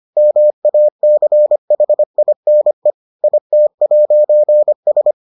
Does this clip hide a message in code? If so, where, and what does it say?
Yes: Morse code from 0.2 to 5.1 s, heard as a keyed tone, about 25 wpm, 600 Hz. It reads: MACHINE IT'S